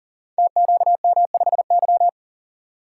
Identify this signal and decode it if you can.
Morse code, heard as a keyed tone, about 37 wpm, 700 Hz. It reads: TQM5Y
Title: Code Groups - 5 Letter:Numbers - 40wpm